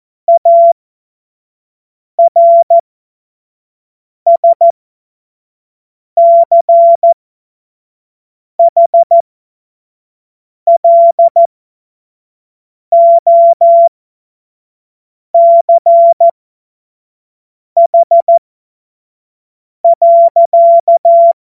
Morse code, heard as a keyed tone, about 14 wpm, spaced as for 5 wpm, 680 Hz. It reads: ARSCHLOCH.
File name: Morsekurs / Witze / Witz 04. April 2023 25 F70